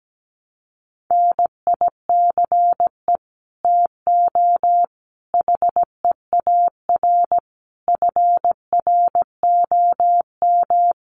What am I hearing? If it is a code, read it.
Morse code, heard as a keyed tone, about 17 wpm, 700 Hz. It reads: NICE TO HEAR FROM
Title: Common QSO Phrases 17wpm